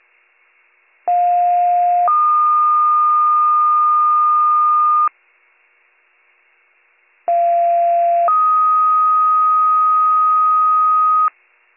Two-tone page, heard as a one-second tone, then a three-second tone, about 710 then 1220 Hz, about 690 then 1250 Hz. It is heard from 1.1 to 11.3 s.